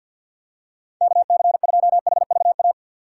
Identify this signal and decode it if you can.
Morse code, heard as a keyed tone, about 53 wpm, 690 Hz. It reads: KC2HVA
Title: USA Call Signs - 6x Character Spacing - 50wpm